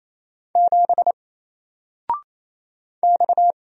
Morse code, heard as a keyed tone, about 28 wpm, 700 Hz. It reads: 7 X
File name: Single-Letter-Number-28wpm